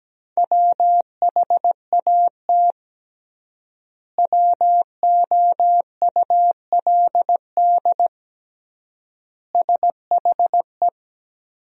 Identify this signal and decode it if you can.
Morse code, heard as a keyed tone, about 17 wpm, 700 Hz. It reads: WHAT WOULD SHE
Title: 3x Word-Spacing - Sentences from Top 100 Words - 17wpm